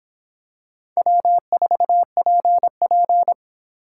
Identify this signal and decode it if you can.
Morse code, heard as a keyed tone, about 26 wpm, 700 Hz. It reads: W4PP